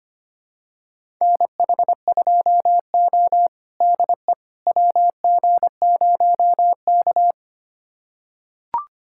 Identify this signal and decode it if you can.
Morse code, heard as a keyed tone, about 25 wpm, 700 Hz. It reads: NH2O DE WG0K
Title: USA Callsign DE Callsign - 4 Characters - 25wpm